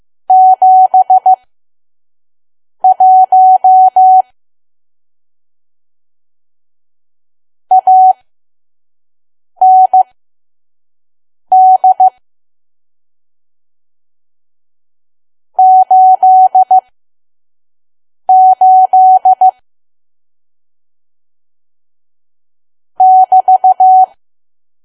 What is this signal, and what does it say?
Morse code, heard as a keyed tone, about 15 wpm, spaced as for 5 wpm, 750 Hz. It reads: 71 AND 88 <BT>
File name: Archive / 5 WPM files / PM